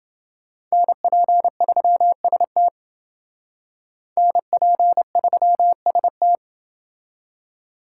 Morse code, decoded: NP3ST NP3ST

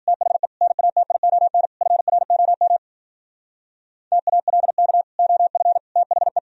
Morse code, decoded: THE NATION FROM TALK OF THE

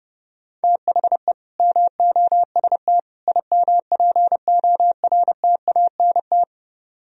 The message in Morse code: THE MOST IMPORTANT

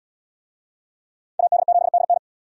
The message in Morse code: AD7DN

